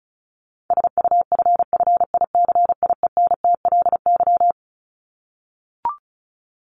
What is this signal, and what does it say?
Morse code, heard as a keyed tone, about 34 wpm, 700 Hz. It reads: SUFFICIENTLY
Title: LY Suffix - Top 100 Words with Base Words 35wpm